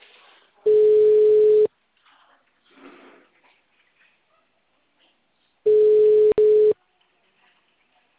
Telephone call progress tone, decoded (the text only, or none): ringback tone